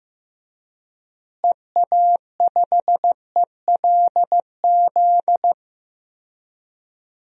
Morse code, heard as a keyed tone, about 15 wpm, 700 Hz. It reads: EA5ELZ
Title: DXCC-100 Call Signs - 6 Characters - Repeated 3x - 15wpm